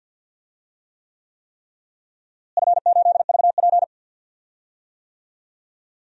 Morse code, heard as a keyed tone, about 53 wpm, 700 Hz. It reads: F8VP
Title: DXCC-100 Call Signs - 4 Characters - Repeated 3x - 50wpm